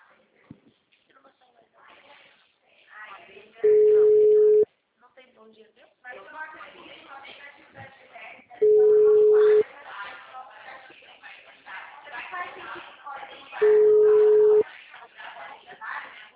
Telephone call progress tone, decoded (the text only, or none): ringback tone